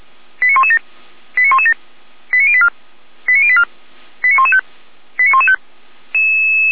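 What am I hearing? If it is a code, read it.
Five-tone selective call (ZVEI), heard as quick runs of five tones, tones of about 80 ms. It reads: 89117 89117 89084 89084 89115 89115
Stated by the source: ZVEI five-tone selective call, code 89325, received by SDR; fire brigade Feuerwehr Orlamünde.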